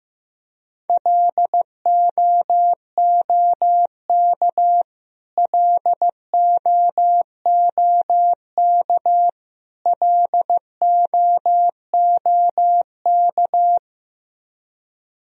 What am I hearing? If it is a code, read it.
Morse code, heard as a keyed tone, about 15 wpm, 700 Hz. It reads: LOOK LOOK LOOK
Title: Top 100 Words Repeated 15wpm